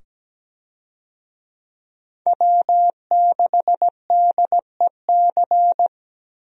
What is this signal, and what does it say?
Morse code, heard as a keyed tone, about 17 wpm, 700 Hz. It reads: W6DEC
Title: USA Call Signs - 5 Characters - 17wpm